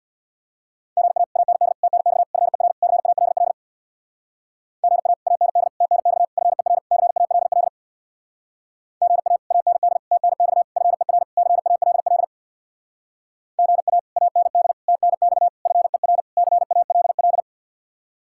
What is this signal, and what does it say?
Morse code, heard as a keyed tone, about 64 wpm, 700 Hz. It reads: GA AND TNX FER CALL GA AND TNX FER CALL GA AND TNX FER CALL GA AND TNX FER CALL